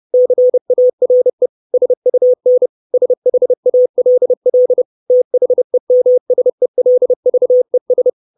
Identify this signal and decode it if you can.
Morse code, heard as a keyed tone, about 30 wpm, 500 Hz. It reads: CARE SUN SHALL THEMSELVES
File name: 632 Words 30WPM